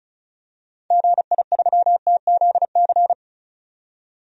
Morse code, heard as a keyed tone, about 34 wpm, 700 Hz. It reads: GI3TZC
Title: DXCC-100 Call Signs - 6 Characters - 35wpm